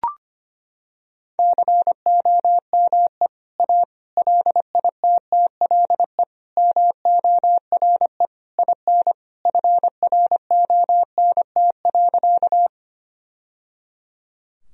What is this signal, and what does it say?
Morse code, heard as a keyed tone, about 25 wpm, 700 Hz. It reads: COME A LITTLE MORE IN FRONT.
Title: Sentences from Top 1000 Words - Review Part 2 - 25wpm